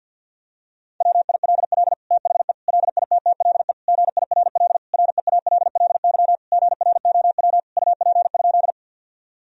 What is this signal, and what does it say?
Morse code, heard as a keyed tone, about 49 wpm, 700 Hz. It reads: WILL THE LITTLE GIRL REALLY GROW UP?
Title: Sentences from Top 1000 Words - Review Part 1 - 50wpm